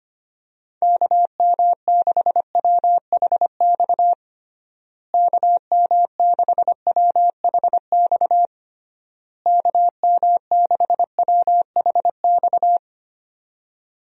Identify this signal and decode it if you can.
Morse code, heard as a keyed tone, about 25 wpm, 700 Hz. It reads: KM6WHX KM6WHX KM6WHX
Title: USA Call Signs - 6 Characters - Repeated 3x - 25wpm